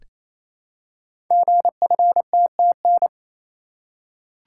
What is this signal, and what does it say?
Morse code, heard as a keyed tone, about 28 wpm, 700 Hz. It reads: GFTTN
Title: Code Groups - 5 Letters - 28wpm